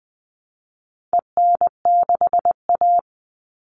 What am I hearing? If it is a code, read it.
Morse code, heard as a keyed tone, about 20 wpm, 700 Hz. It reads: EN6A
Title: All Cut Numbers - 4-Digit 20wpm